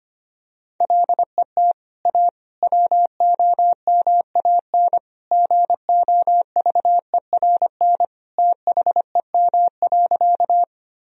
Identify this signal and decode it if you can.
Morse code, heard as a keyed tone, about 25 wpm, 700 Hz. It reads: LET A WOMAN GOVERN THEM.
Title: Sentences from Top 1000 Words - Review Part 2 - 25wpm